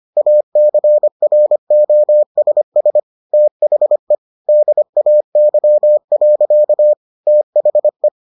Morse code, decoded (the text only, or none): ACROSS THE DAY. THE